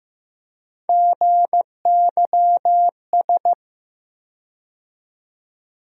Morse code, heard as a keyed tone, about 15 wpm, 700 Hz. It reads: GYS
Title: USA Call Sign Postfixes - 3 Letters 15wpm